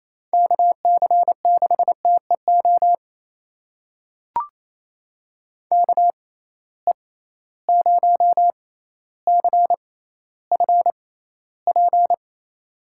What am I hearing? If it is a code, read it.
Morse code, heard as a keyed tone, about 28 wpm, spaced as for 14 wpm, 700 Hz. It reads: KC6TEO KE0CFP
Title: USA Call Signs - 6 Characters - 6x Character Spacing - 28wpm